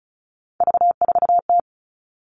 Morse code, heard as a keyed tone, about 34 wpm, 700 Hz. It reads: V4T
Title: All Cut Numbers - 3-Digit 35wpm